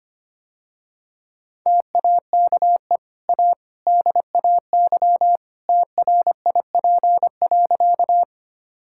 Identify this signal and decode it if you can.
Morse code, heard as a keyed tone, about 25 wpm, 700 Hz. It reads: TAKE A DAY TRIP.
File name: Easy Sentences with Hints - 25wpm